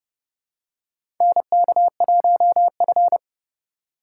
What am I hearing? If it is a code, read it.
Morse code, heard as a keyed tone, about 32 wpm, 700 Hz. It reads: NK1F
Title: USA Call Signs - 3x Character Spacing - 30wpm